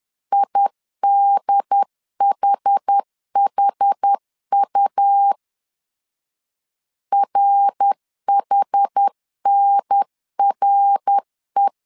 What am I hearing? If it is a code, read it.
Morse code, heard as a keyed tone, about 10 wpm, 800 Hz. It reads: IDHHU RHNRE
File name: 10wpm